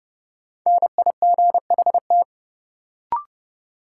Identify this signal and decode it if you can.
Morse code, heard as a keyed tone, about 30 wpm, 700 Hz. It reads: NIGHT
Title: Sets of 4 Words - New Word Drill - 30wpm